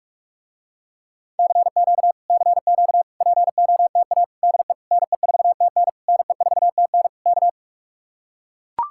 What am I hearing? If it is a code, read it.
Morse code, heard as a keyed tone, about 44 wpm, 700 Hz. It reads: CQ CQ POTA DE NE4TN NE4TN K